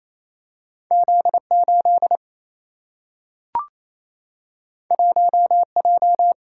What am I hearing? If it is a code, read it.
Morse code, heard as a keyed tone, about 28 wpm, 700 Hz. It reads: Z 8 1 J
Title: Single-Letter Single-Number 28wpm